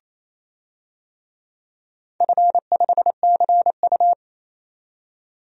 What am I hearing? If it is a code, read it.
Morse code, heard as a keyed tone, about 28 wpm, 700 Hz. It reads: F5CU
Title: DXCC-100 Call Signs - 4 Characters - 28wpm